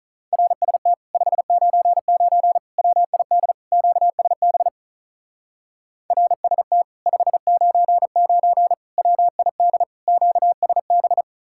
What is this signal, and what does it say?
Morse code, heard as a keyed tone, about 43 wpm, 700 Hz. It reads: RST 599 WID QSB RST 599 WID QSB
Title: Speed-Racing Common QSO Phrases Encore - 35wpm